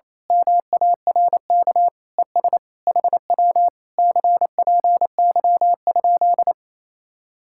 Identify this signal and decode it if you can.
Morse code, heard as a keyed tone, about 26 wpm, 700 Hz. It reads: MARK ES HW CPY?
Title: Speed-Racing Common QSO Phrases - 28wpm